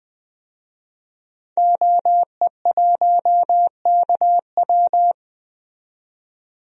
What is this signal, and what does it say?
Morse code, heard as a keyed tone, about 20 wpm, 700 Hz. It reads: OE1KW
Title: DXCC-100 Call Signs - 5 Characters - 20wpm